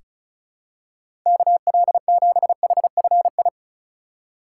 Morse code, heard as a keyed tone, about 36 wpm, 700 Hz. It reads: KL7HFI
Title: K1USN SST QSO Practice - 35wpm